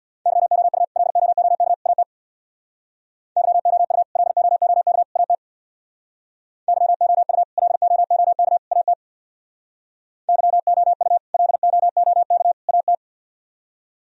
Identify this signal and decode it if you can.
Morse code, heard as a keyed tone, about 64 wpm, 700 Hz. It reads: YOU LOOK AT YOU LOOK AT YOU LOOK AT YOU LOOK AT